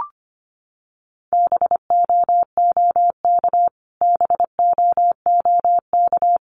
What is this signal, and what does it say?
Morse code, heard as a keyed tone, about 25 wpm, 700 Hz. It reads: BOOK BOOK
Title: Top 500 Words Repeated 25wpm